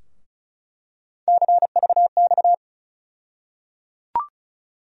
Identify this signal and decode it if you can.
Morse code, heard as a keyed tone, about 34 wpm, 700 Hz. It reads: CVX
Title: USA Call Sign Postfixes - 3 Letters 35wpm